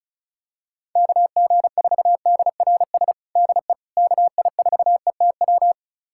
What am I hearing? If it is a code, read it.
Morse code, heard as a keyed tone, about 34 wpm, 700 Hz. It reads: KG4DRS DE KI4ETW
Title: USA Callsign DE Callsign - 6 Characters - 35wpm